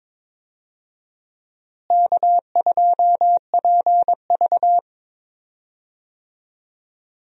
Morse code, heard as a keyed tone, about 21 wpm, 700 Hz. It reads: K2PV
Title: USA Call Signs - 4 Characters - Repeated 3x - 22wpm